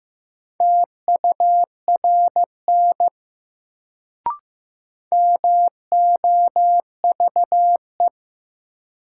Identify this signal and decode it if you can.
Morse code, heard as a keyed tone, about 15 wpm, 700 Hz. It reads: TURN MOVE
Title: Sentences from Top 200 Words - New Word Drill - 15wpm